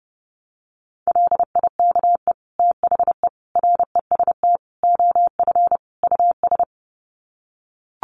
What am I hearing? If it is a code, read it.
Morse code, heard as a keyed tone, about 30 wpm, 700 Hz. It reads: LIKE THE REST OF US
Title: Sets of 5 Words Encore2 30wpm